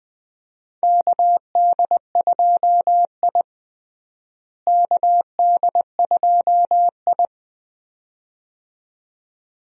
Morse code, heard as a keyed tone, about 20 wpm, 700 Hz. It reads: KD2I KD2I
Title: USA Call Signs - 4 Characters - Repeated 2x - 20wpm